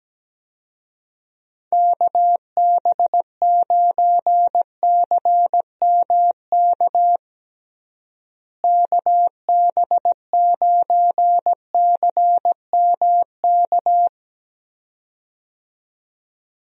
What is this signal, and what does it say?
Morse code, heard as a keyed tone, about 17 wpm, 700 Hz. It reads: KB9CMK KB9CMK
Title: USA Call Signs - 6 Characters - Repeated 2x - 17wpm